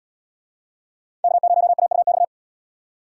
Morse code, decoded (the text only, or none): K0AUX